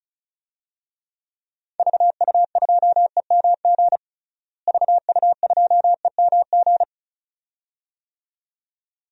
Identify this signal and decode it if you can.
Morse code, heard as a keyed tone, about 36 wpm, 700 Hz. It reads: VU2EMG VU2EMG
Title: DXCC-100 Call Signs - 6 Characters - Repeated 2x - 35wpm